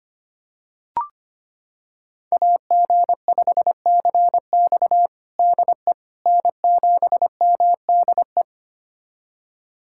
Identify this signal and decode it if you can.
Morse code, heard as a keyed tone, about 25 wpm, 700 Hz. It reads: AG5CX DE N7MDE
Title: USA Callsign DE Callsign - 5 Characters - 25wpm